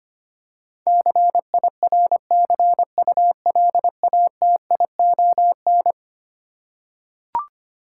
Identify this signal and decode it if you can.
Morse code, heard as a keyed tone, about 25 wpm, 700 Hz. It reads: CIRCULATION
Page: morse-code-ninja-practice-25wpm